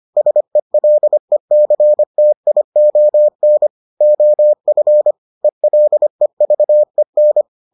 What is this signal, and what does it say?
Morse code, heard as a keyed tone, about 25 wpm, 600 Hz. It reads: SELECTION OF ELEVEN